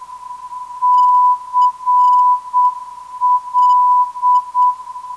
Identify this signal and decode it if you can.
Morse code, heard as a keyed tone, about 7 wpm, 990 Hz. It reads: CL
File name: CL - 445.68kHz